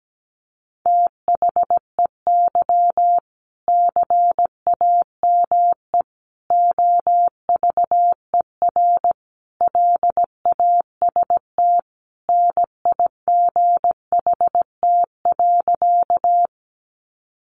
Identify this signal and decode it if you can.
Morse code, heard as a keyed tone, about 17 wpm, 700 Hz. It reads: THEY CAME OVER LAST NIGHT.
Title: Sentences from Top 300 Words 17wpm